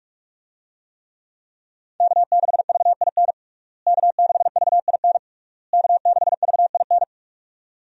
Morse code, heard as a keyed tone, about 41 wpm, 700 Hz. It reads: K6VIN K6VIN K6VIN
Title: USA Call Signs - 5 Characters - Repeated 3x - 45wpm